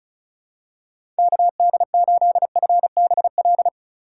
Morse code, heard as a keyed tone, about 34 wpm, 700 Hz. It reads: KD8FBL
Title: USA Call Signs - 3x Character Spacing - 35wpm